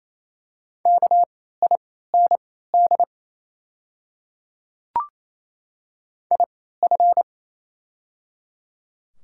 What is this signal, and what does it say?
Morse code, heard as a keyed tone, about 27 wpm, spaced as for 16 wpm, 690 Hz. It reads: KIND IF
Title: 3x Character Spacing - Top 300 Words - 28wpm